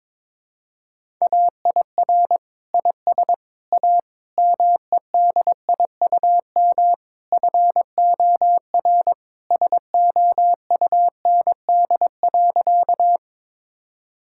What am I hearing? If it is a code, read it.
Morse code, heard as a keyed tone, about 21 wpm, 700 Hz. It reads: AIR IS A MEDIUM FOR SOUND.
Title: Sentences from Top 1000 Words - Review Part 6 - 22wpm